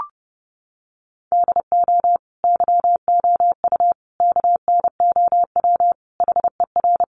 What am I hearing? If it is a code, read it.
Morse code, heard as a keyed tone, about 30 wpm, 700 Hz. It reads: DO YOU KNOW HER
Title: Sentences from Top 1400 Words 30wpm